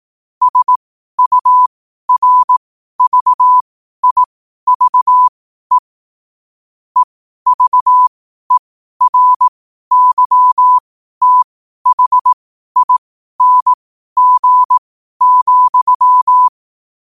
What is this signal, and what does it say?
Morse code, heard as a keyed tone, about 18 wpm, spaced as for 12 wpm, 1000 Hz. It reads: SURVIVE EVERYTHING,
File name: qotdfiles / 12wpm